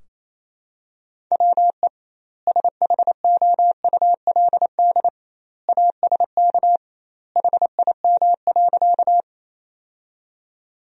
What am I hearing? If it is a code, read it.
Morse code, heard as a keyed tone, about 27 wpm, 700 Hz. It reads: WE SHOULD ASK HIM.
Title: 2x Word-Spacing - Sentences from Top 300 Words - 28wpm